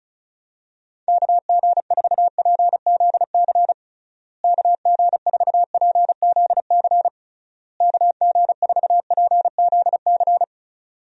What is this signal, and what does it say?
Morse code, heard as a keyed tone, about 34 wpm, 700 Hz. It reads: KG4PZC KG4PZC KG4PZC
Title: USA Call Signs - 6 Characters - Repeated 3x - 35wpm